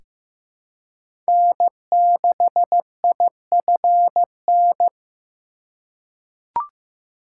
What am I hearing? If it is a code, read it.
Morse code, heard as a keyed tone, about 15 wpm, 700 Hz. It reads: N6IFN
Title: USA Call Signs - 3x Character Spacing - 15wpm